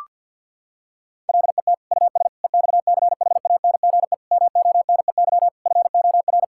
Morse code, heard as a keyed tone, about 46 wpm, 700 Hz. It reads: LET US EXCHANGE MONEY FOR
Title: Sentences from Top 1000 Words - Review Part 6 - 50wpm